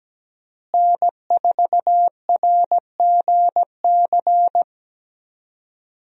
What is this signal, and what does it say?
Morse code, heard as a keyed tone, about 17 wpm, 700 Hz. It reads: N4RGC
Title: USA Call Signs - 5 Characters - Repeated 3x - 17wpm